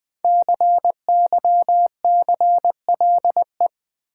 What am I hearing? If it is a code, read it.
Morse code, heard as a keyed tone, about 21 wpm, 700 Hz. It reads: CYCLE
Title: Sentences from Top 900 Words - New Word Drill - 20wpm